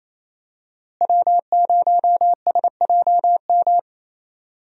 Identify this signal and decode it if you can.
Morse code, heard as a keyed tone, about 29 wpm, 700 Hz. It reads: W0SJM
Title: USA Call Signs - 5 Characters - Repeated 3x - 28wpm